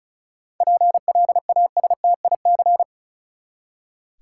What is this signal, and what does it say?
Morse code, heard as a keyed tone, about 36 wpm, 700 Hz. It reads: PLASTIC
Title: Seven Letter Words - 35wpm